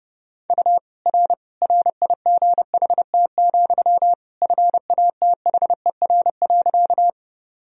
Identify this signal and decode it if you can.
Morse code, heard as a keyed tone, about 32 wpm, 700 Hz. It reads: U R RIGHT, FATHER.
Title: Sentences from Top 300 Words and Abbreviations 30wpm